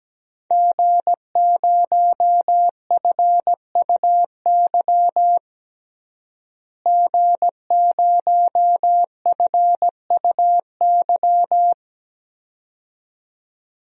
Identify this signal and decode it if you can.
Morse code, heard as a keyed tone, about 17 wpm, 700 Hz. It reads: G0FUY G0FUY
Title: DXCC-100 Call Signs - 5 Characters - Repeated 2x - 17wpm